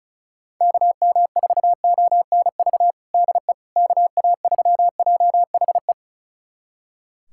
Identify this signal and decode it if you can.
Morse code, heard as a keyed tone, about 36 wpm, 690 Hz. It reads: KM4ONV DE KA3JHE
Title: USA Callsign DE Callsign - 6 Characters - 35wpm